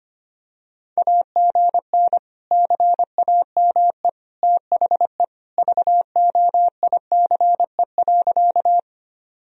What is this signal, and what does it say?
Morse code, heard as a keyed tone, about 25 wpm, 700 Hz. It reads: AGN CAME THE VOICE.
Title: Sentences from Top 500 Words and Abbreviations 25wpm